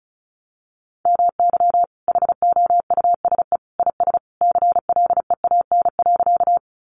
Morse code, heard as a keyed tone, about 34 wpm, 700 Hz. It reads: MY HOUSE IS CLEAN.
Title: Sentences from Top 1100 Words 35wpm